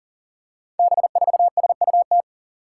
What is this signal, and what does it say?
Morse code, heard as a keyed tone, about 39 wpm, 700 Hz. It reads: B4SUT
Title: Five Challenging Letter-Number Combinations 40wpm